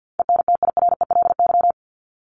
Morse code, heard as a keyed tone, about 50 wpm, 700 Hz. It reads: ENTIRELY
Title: LY Suffix - Top 100 Words with Base Words 50wpm